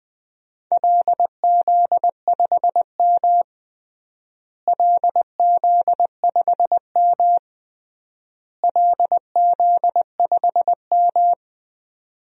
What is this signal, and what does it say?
Morse code, heard as a keyed tone, about 20 wpm, 700 Hz. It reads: LZ5M LZ5M LZ5M
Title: DXCC-100 Call Signs - 4 Characters - Repeated 3x - 20wpm